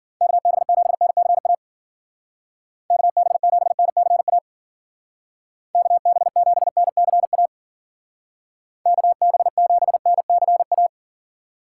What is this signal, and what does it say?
Morse code, heard as a keyed tone, about 52 wpm, 700 Hz. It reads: KB7NCA KB7NCA KB7NCA KB7NCA